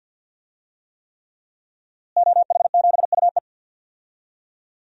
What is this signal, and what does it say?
Morse code, heard as a keyed tone, about 47 wpm, 700 Hz. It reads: OH7UE
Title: DXCC-100 Call Signs - 5 Characters - Repeated 3x - 50wpm